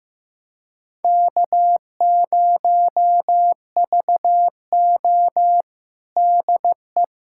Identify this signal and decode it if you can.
Morse code, heard as a keyed tone, about 16 wpm, 700 Hz. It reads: K0VO DE
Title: USA Callsign DE Callsign - 4 Characters - 15wpm